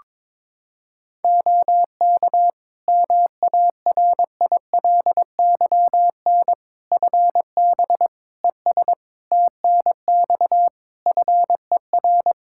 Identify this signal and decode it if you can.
Morse code, heard as a keyed tone, about 22 wpm, 700 Hz. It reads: OK MARILYN FB ES TNX FER
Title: Practice QSO - Exchange 2 - Info - 22wpm